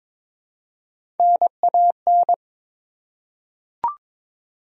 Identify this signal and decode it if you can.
Morse code, heard as a keyed tone, about 22 wpm, 700 Hz. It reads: NAN
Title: USA Common Names - Max 3 Letters - 22wpm